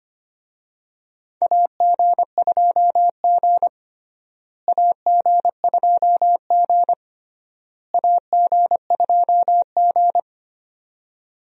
Morse code, heard as a keyed tone, about 25 wpm, 700 Hz. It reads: AG2G AG2G AG2G
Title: USA Call Signs - 4 Characters - Repeated 3x - 25wpm